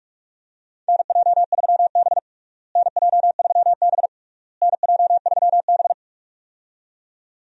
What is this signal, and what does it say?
Morse code, heard as a keyed tone, about 47 wpm, 700 Hz. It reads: NJ3B NJ3B NJ3B
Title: USA Call Signs - 4 Characters - Repeated 3x - 45wpm